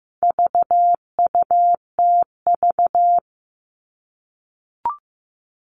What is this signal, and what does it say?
Morse code, heard as a keyed tone, about 15 wpm, 700 Hz. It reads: VUTV
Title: All Cut Numbers - 4-Digit 15wpm